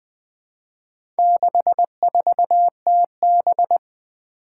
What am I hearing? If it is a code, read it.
Morse code, heard as a keyed tone, about 20 wpm, 700 Hz. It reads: 64TB
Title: Four Challenging Letter-Number Combinations 20wpm